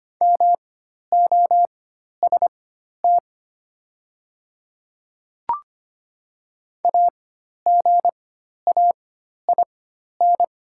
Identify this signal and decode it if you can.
Morse code, heard as a keyed tone, about 25 wpm, spaced as for 12 wpm, 700 Hz. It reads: MOST AGAIN
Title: 4x Character Spacing - Top 300 Words - 25wpm